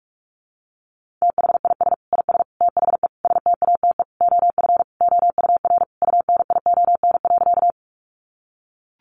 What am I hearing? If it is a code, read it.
Morse code, heard as a keyed tone, about 44 wpm, 700 Hz. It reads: THIS IS THE STATE OF OUR UNION.
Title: Sentences from Top 1400 Words 45wpm